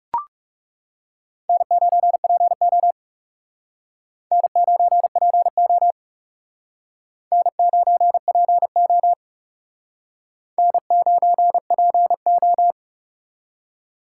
Morse code, heard as a keyed tone, about 40 wpm, 700 Hz. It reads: N9PO N9PO N9PO N9PO